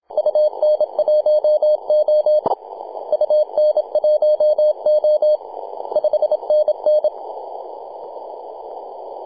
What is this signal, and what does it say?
Morse code, heard as a keyed tone, about 26 wpm, 610 Hz. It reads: UN1O UN1O 5NN
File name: t88zo-28-008mhzcw